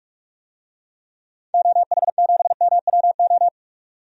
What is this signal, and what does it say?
Morse code, heard as a keyed tone, about 46 wpm, 700 Hz. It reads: OH7MWO
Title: DXCC-100 Call Signs - 6 Characters - 45wpm